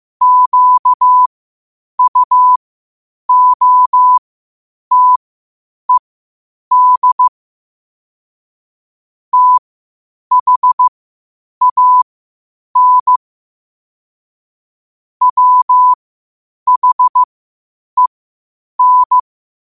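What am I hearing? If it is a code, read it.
Morse code, heard as a keyed tone, about 15 wpm, spaced as for 8 wpm, 1000 Hz. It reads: QUOTED THAN WHEN
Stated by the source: Morse code quotes at 8 words per minute from QOTD at 8 WPM.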